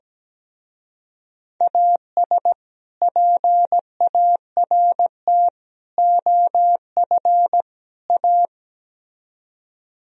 Morse code, heard as a keyed tone, about 17 wpm, 700 Hz. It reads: AS PART OF A
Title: Speed-Racing - Sets of 4 Words 17wpm